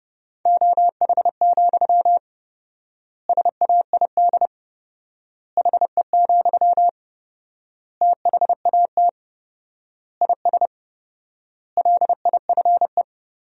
Morse code, heard as a keyed tone, about 32 wpm, 700 Hz. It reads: OH, SAID HE, THAT IS LIFE
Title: 4x Word-Spacing - Sentences from Top 500 Words - 30wpm